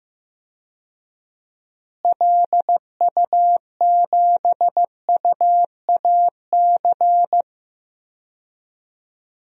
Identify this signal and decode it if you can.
Morse code, heard as a keyed tone, about 15 wpm, 700 Hz. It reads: LU7UAC
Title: DXCC-100 Call Signs - 6 Characters - Repeated 3x - 15wpm